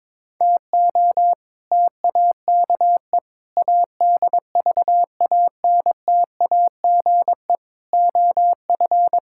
Morse code, heard as a keyed tone, about 22 wpm, 700 Hz. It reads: TO TAKE ADVANTAGE OF
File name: practice / 22wpm / Sentences with 4-Word Phrases - 22wpm